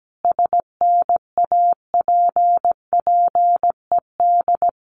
Morse code, heard as a keyed tone, about 17 wpm, 700 Hz. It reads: SNAPPED
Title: A Sound Of Thunder 17wpm